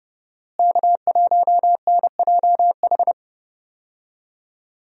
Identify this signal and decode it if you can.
Morse code, heard as a keyed tone, about 31 wpm, 700 Hz. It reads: K1NJH
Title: USA Call Signs - 5 Characters - Repeated 3x - 30wpm